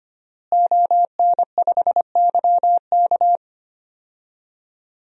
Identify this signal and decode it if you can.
Morse code, heard as a keyed tone, about 25 wpm, 700 Hz. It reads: ON5YK
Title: DXCC-100 Call Signs - 5 Characters - Repeated 2x - 25wpm